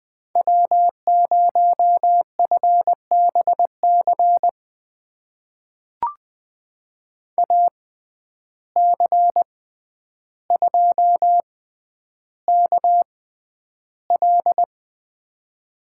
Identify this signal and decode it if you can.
Morse code, heard as a keyed tone, about 20 wpm, spaced as for 10 wpm, 700 Hz. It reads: W0FBC AC2KL
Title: USA Call Signs - 5 Characters - 6x Character Spacing - 20wpm